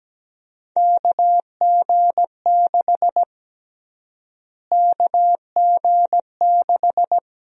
Morse code, heard as a keyed tone, about 17 wpm, 700 Hz. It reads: KG6 KG6
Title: USA Call Sign Prefixes - Repeated 2x - 17wpm